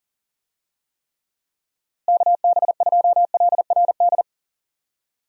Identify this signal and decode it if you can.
Morse code, heard as a keyed tone, about 40 wpm, 700 Hz. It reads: KB2LRD
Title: USA Call Signs - 6 Characters - Repeated 3x - 40wpm